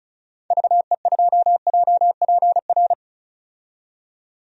Morse code, heard as a keyed tone, about 36 wpm, 700 Hz. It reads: VE2JPR
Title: DXCC-100 Call Signs - 6 Characters - 35wpm